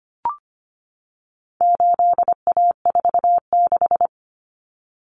Morse code, heard as a keyed tone, about 25 wpm, 700 Hz. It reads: 8A46
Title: Common Cut Numbers - 4-Digit 25wpm